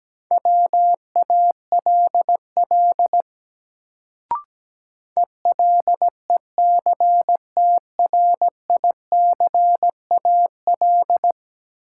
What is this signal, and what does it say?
Morse code, heard as a keyed tone, about 17 wpm, 700 Hz. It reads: WALL ELECTRICAL
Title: Sentences from Top 1000 Words - Review Part 6 - New Word Drill - 17wpm